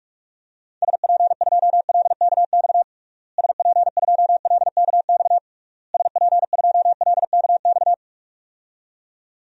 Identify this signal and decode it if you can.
Morse code, heard as a keyed tone, about 41 wpm, 700 Hz. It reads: SP2LKX SP2LKX SP2LKX
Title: DXCC-100 Call Signs - 6 Characters - Repeated 3x - 45wpm